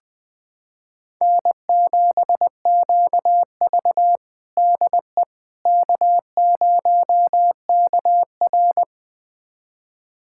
Morse code, heard as a keyed tone, about 20 wpm, 700 Hz. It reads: N7QV DE K0KR
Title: USA Callsign DE Callsign - 4 Characters - 20wpm